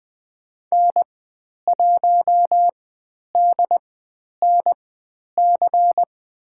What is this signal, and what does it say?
Morse code, heard as a keyed tone, about 20 wpm, spaced as for 10 wpm, 700 Hz. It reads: N1DNC
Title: 095 Lesson 16g - Letter c - Call Signs 20wpm